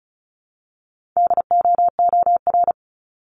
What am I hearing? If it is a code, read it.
Morse code, heard as a keyed tone, about 34 wpm, 700 Hz. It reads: DOOR
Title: Top 300 Words 35wpm